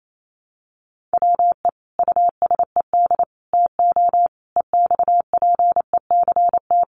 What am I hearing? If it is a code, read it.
Morse code, heard as a keyed tone, about 28 wpm, 700 Hz. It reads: WE USED TO EXPECT
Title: Sentences from Top 1100 Words 28wpm